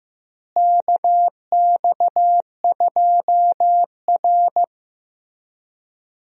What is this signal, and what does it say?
Morse code, heard as a keyed tone, about 15 wpm, 700 Hz. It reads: KX2R
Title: USA Call Signs - 4 Characters - Repeated 3x - 15wpm